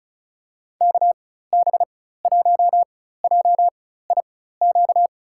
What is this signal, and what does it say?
Morse code, heard as a keyed tone, about 36 wpm, spaced as for 16 wpm, 700 Hz. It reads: KB1JIQ
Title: USA Call Signs - 4x Character Spacing - 35wpm